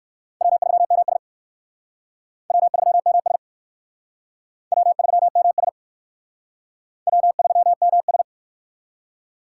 Morse code, heard as a keyed tone, about 62 wpm, 700 Hz. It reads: W3MS W3MS W3MS W3MS